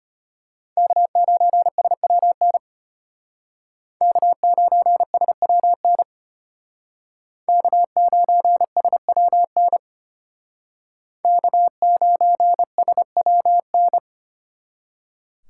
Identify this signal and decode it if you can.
Morse code, heard as a keyed tone, about 33 wpm, 700 Hz. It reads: K9SWN K9SWN K9SWN K9SWN